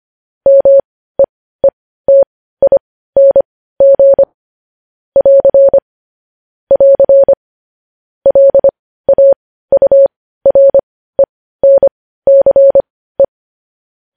Morse code, heard as a keyed tone, about 25 wpm, spaced as for 15 wpm, 550 Hz. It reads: MEETING <AR> <AR> LAURENCE